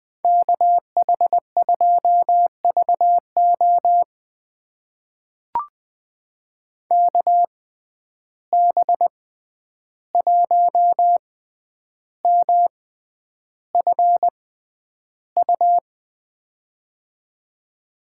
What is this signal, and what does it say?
Morse code, heard as a keyed tone, about 20 wpm, spaced as for 10 wpm, 700 Hz. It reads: KH2VO KB1MFU
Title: USA Call Signs - 6x Character Spacing - 20wpm